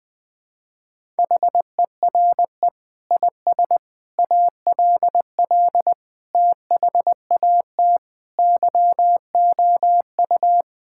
Morse code, heard as a keyed tone, about 20 wpm, 700 Hz. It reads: HERE IS ALL THAT YOU
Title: Sentences from Top 1000 Words - Review Part 6 - 20wpm